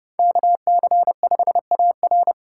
Morse code, heard as a keyed tone, about 30 wpm, 700 Hz. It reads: KC5AR